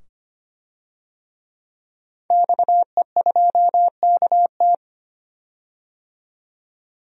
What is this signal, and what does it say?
Morse code, heard as a keyed tone, about 25 wpm, 700 Hz. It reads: XE2KT
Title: DXCC-100 Call Signs - 5 Characters - 25wpm